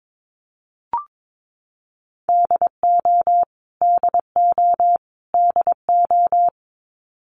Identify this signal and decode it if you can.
Morse code, heard as a keyed tone, about 22 wpm, 700 Hz. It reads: DO DO DO
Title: Top 500 Words Repeated 22wpm